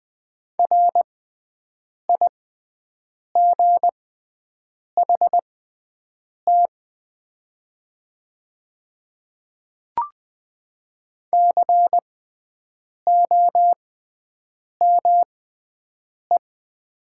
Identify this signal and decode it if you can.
Morse code, heard as a keyed tone, about 20 wpm, spaced as for 7 wpm, 700 Hz. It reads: RIGHT COME
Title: 6x Character Spacing - Top 200 Words - 20wpm